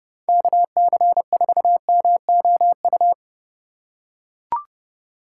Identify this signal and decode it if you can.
Morse code, heard as a keyed tone, about 30 wpm, 700 Hz. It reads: KC4MOU
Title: USA Call Signs - 6 Characters - 4x Character Spacing - 30wpm